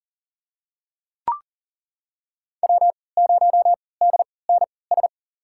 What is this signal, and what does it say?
Morse code, heard as a keyed tone, about 40 wpm, spaced as for 23 wpm, 700 Hz. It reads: W0DNS
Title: USA Call Signs - 5 Characters - 3x Character Spacing - 40wpm